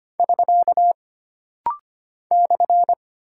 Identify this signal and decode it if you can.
Morse code, heard as a keyed tone, about 25 wpm, 700 Hz. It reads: <SK> /
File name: Single Punctuation-Prosign - Rapid-Fire - 25wpm